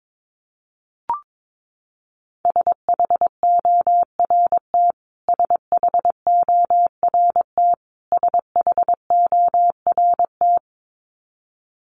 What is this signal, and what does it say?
Morse code, heard as a keyed tone, about 22 wpm, 700 Hz. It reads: SHORT SHORT SHORT
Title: Top 500 Words Repeated 22wpm